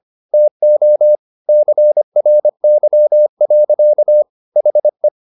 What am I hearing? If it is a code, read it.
Morse code, heard as a keyed tone, about 25 wpm, 600 Hz. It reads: TO CRY. HE